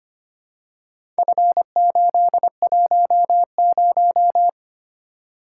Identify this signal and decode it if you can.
Morse code, heard as a keyed tone, about 25 wpm, 700 Hz. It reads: F810
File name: practice / 25wpm / Four Letter-Number Combinations - 25wpm